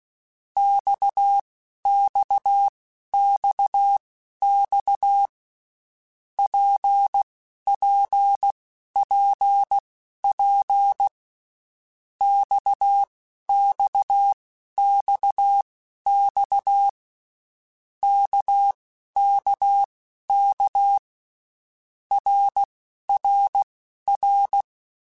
Morse code, heard as a keyed tone, about 16 wpm, spaced as for 12 wpm, 800 Hz. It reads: XXXX PPPP XXXX KKK RRR
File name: Basic Letter Groups By7Groups / RKPX / RKPX@10